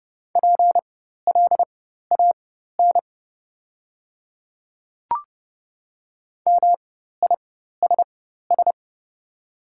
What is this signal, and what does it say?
Morse code, heard as a keyed tone, about 30 wpm, spaced as for 14 wpm, 700 Hz. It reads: PLAN MISS
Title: 4x Character Spacing - Top 500 Words - 30wpm